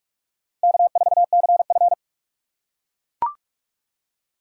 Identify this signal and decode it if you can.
Morse code, heard as a keyed tone, about 42 wpm, 700 Hz. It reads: K4CF
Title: USA Call Signs - 6x Character Spacing - 45wpm